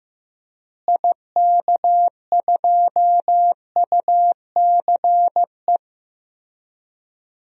Morse code, heard as a keyed tone, about 15 wpm, 700 Hz. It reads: IK2UCE
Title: DXCC-100 Call Signs - 6 Characters - Repeated 3x - 15wpm